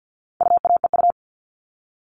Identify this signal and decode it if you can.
Morse code, heard as a keyed tone, about 50 wpm, 700 Hz. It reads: UAEU
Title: All Cut Numbers - 4-Digit 50wpm